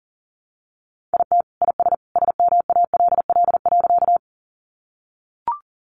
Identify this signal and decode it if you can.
Morse code, heard as a keyed tone, about 40 wpm, 700 Hz. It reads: IT IS SMALL.
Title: Easy Phrases 40wpm